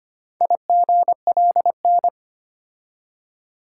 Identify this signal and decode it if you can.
Morse code, heard as a keyed tone, about 25 wpm, 700 Hz. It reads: IGLN
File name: practice / 25wpm / Four Letter-Number Combinations - 25wpm